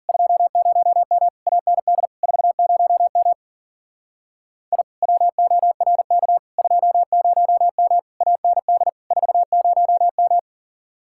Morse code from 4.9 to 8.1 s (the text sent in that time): WORK 20M